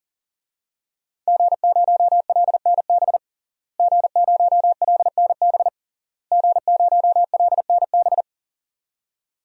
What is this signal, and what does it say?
Morse code, heard as a keyed tone, about 40 wpm, 700 Hz. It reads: G0LNB G0LNB G0LNB
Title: DXCC-100 Call Signs - 5 Characters - Repeated 3x - 40wpm